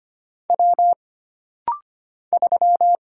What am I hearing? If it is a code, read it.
Morse code, heard as a keyed tone, about 25 wpm, 700 Hz. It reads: W 3
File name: Single Character (Letter:Number:Punctuation:Prosign) - Rapid-Fire - 25wpm